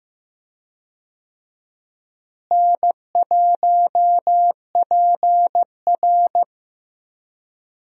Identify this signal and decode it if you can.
Morse code, heard as a keyed tone, about 15 wpm, 700 Hz. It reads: N1PR
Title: USA Call Signs - 4 Characters - Repeated 2x - 15wpm